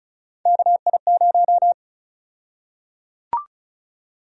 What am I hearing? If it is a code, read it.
Morse code, heard as a keyed tone, about 34 wpm, 700 Hz. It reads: KI0
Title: USA Call Sign Prefixes - 35wpm